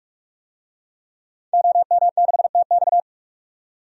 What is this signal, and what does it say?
Morse code, heard as a keyed tone, about 42 wpm, 700 Hz. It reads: OM6TX
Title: DXCC-100 Call Signs - 5 Characters - 45wpm